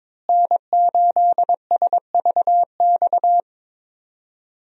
Morse code, heard as a keyed tone, about 22 wpm, 700 Hz. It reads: N8SVX
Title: USA Call Signs - 5 Characters - Repeated 2x - 22wpm